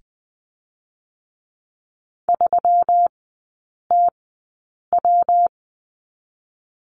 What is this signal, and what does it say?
Morse code, heard as a keyed tone, about 20 wpm, spaced as for 8 wpm, 700 Hz. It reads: 3TW
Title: 121 Lesson 20e - Number 3 - Three Characters 20wpm